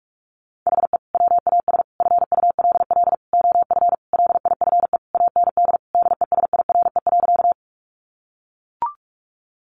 Morse code, heard as a keyed tone, about 44 wpm, 700 Hz. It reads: HE WAS FULL OF LIFE AND DESIRE.
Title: Sentences from Top 1100 Words 45wpm